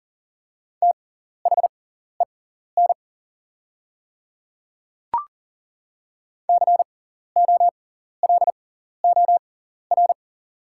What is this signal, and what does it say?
Morse code, heard as a keyed tone, about 40 wpm, spaced as for 14 wpm, 690 Hz. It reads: THEN COLOR
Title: 6x Character Spacing - Top 300 Words - 40wpm